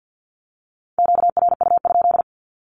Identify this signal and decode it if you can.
Morse code, heard as a keyed tone, about 50 wpm, 700 Hz. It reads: QRU?
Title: Q-Codes with Spoken Meaning - 50wpm